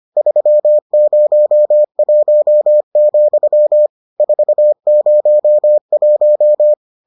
Morse code, heard as a keyed tone, about 25 wpm, 600 Hz. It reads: 301, 401